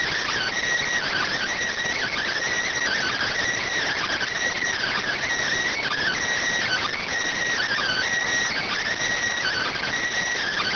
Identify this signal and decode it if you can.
Emergency Alert System (SAME) tones (header bursts), heard as harsh buzzing bursts. It is heard from 1.0 to 9.9 s.